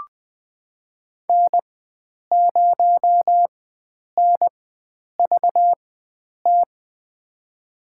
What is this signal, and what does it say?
Morse code, heard as a keyed tone, about 20 wpm, spaced as for 9 wpm, 700 Hz. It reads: N0NVT
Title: USA Call Signs - 5 Characters - 4x Character Spacing - 20wpm